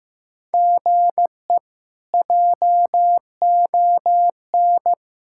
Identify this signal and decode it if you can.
Morse code, heard as a keyed tone, about 15 wpm, 700 Hz. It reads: GE JON
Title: K1USN SST QSO Practice - 15wpm